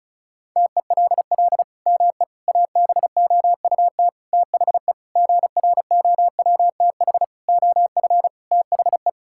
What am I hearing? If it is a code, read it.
Morse code, heard as a keyed tone, about 36 wpm, 700 Hz. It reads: TELL ME ABOUT THE GROWTH OF THE
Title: Sentences from Top 1000 Words - Review Part 5 - 35wpm